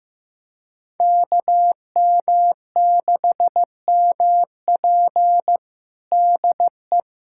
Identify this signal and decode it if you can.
Morse code, heard as a keyed tone, about 15 wpm, 700 Hz. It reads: KM6MP DE
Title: USA Callsign DE Callsign - 5 Characters - 15wpm